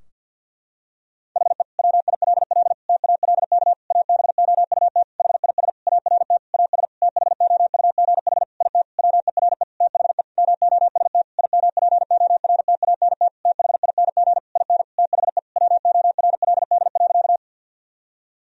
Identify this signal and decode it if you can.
Morse code, heard as a keyed tone, about 52 wpm, 700 Hz. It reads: HE WILL TALK ABOUT HIS ART AS THOUGH IT WERE THE MOST IMPORTANT THING IN THE WORLD.
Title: Sentences from Top 1000 Words - Review Part 3 - 50wpm